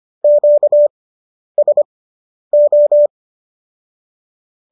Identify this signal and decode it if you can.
Morse code, heard as a keyed tone, about 25 wpm, spaced as for 10 wpm, 590 Hz. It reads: QSO